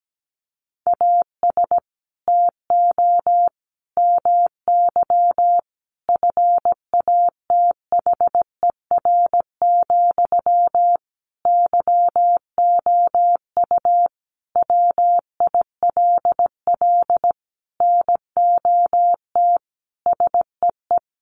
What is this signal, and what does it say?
Morse code, heard as a keyed tone, about 17 wpm, 700 Hz. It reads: AS TO MY FATHER, YOU WILL NOT SEE
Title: Sentences from Top 600 Words 17wpm